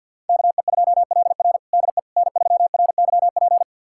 Morse code, heard as a keyed tone, about 52 wpm, 700 Hz. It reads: KE2LR DE N3RYP
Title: USA Callsign DE Callsign - 5 Characters - 50wpm